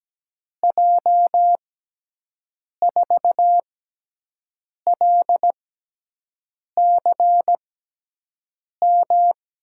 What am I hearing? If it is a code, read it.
Morse code, heard as a keyed tone, about 17 wpm, spaced as for 6 wpm, 700 Hz. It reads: J4LCM
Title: USA Call Signs - 6x Character Spacing - 17wpm